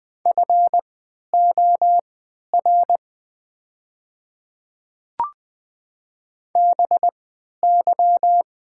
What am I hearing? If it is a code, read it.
Morse code, heard as a keyed tone, about 20 wpm, spaced as for 11 wpm, 700 Hz. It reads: FOR BY
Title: 3x Character Spacing - Top 100 Words - 20wpm